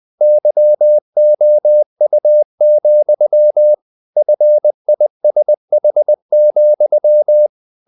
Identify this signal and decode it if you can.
Morse code, heard as a keyed tone, about 20 wpm, 600 Hz. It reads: YOU, FISH,
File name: the old man and the sea / the old man and the sea part 1 20wpm